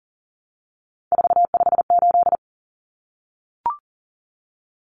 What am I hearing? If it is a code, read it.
Morse code, heard as a keyed tone, about 40 wpm, 700 Hz. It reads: 458
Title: Numbers - 3-Digit 40wpm